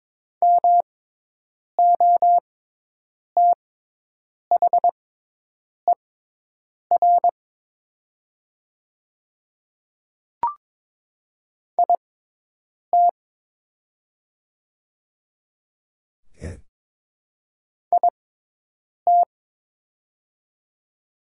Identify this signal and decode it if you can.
Morse code, heard as a keyed tone, about 22 wpm, spaced as for 8 wpm, 700 Hz. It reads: MOTHER IT IT
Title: 6x Character Spacing - Top 200 Words - 22wpm